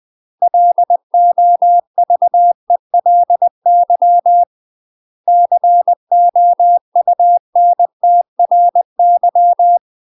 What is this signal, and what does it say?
Morse code, heard as a keyed tone, about 20 wpm, 700 Hz. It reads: LOVELY COUNTRY
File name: UK MASTER QSOs TRAINING / 20WPM / UK MASTER QSOs 0058